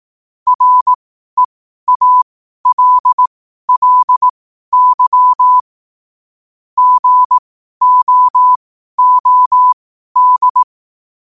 Morse code, heard as a keyed tone, about 18 wpm, spaced as for 12 wpm, 1000 Hz. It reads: REALLY GOOD
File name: qotdfiles / 12wpm